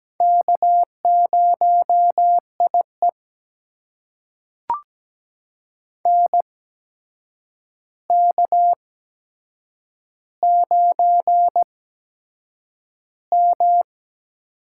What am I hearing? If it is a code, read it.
Morse code, heard as a keyed tone, about 17 wpm, spaced as for 7 wpm, 700 Hz. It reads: K0IE NK9M
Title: USA Call Signs - 4 Characters - 8x Character Spacing - 17wpm